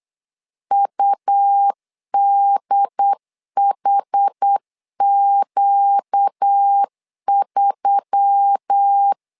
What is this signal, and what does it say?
Morse code, heard as a keyed tone, about 8 wpm, 800 Hz. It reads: UDHQ3
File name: LEZIONI / 4 Q1S3 8wpm 3037